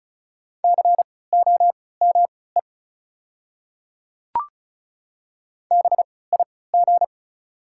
Morse code, heard as a keyed tone, about 34 wpm, spaced as for 20 wpm, 700 Hz. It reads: COME BIG
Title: 3x Character Spacing - Top 500 Words - 35wpm